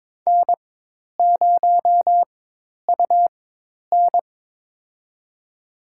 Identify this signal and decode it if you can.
Morse code, heard as a keyed tone, about 22 wpm, spaced as for 10 wpm, 700 Hz. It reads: N0UN
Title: USA Call Signs - 4x Character Spacing - 22wpm